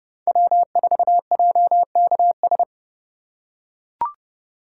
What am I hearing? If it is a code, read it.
Morse code, heard as a keyed tone, about 32 wpm, 700 Hz. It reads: W4JKS
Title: USA Call Signs - 3x Character Spacing - 30wpm